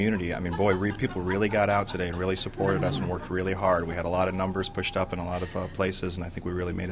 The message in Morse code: none

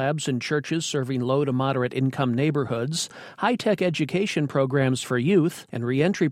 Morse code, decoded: none

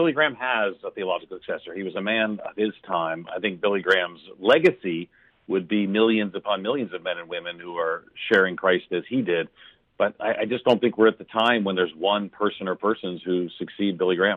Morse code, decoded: none

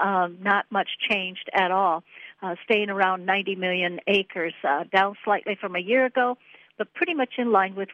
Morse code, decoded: none

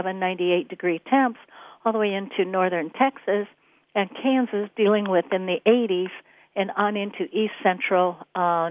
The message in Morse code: none

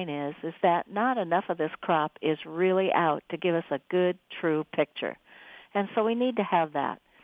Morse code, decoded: none

